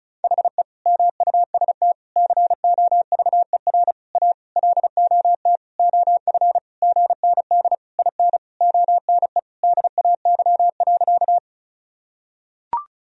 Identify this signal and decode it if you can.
Morse code, heard as a keyed tone, about 36 wpm, 700 Hz. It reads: HE MUST COVER A LOT OF GND IN ONE DAY.
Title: Sentences from Top 500 Words and Abbreviations 35wpm